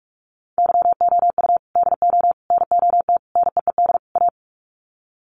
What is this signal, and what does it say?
Morse code, heard as a keyed tone, about 44 wpm, 700 Hz. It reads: YOU DO NOT NEED A
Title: Sets of 5 Words Encore 45wpm